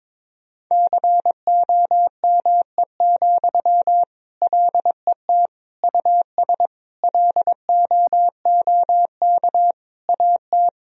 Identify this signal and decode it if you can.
Morse code, heard as a keyed tone, about 22 wpm, 700 Hz. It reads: COME, LET US LOOK AT
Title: Sentences from Top 1000 Words - Review Part 3 - 22wpm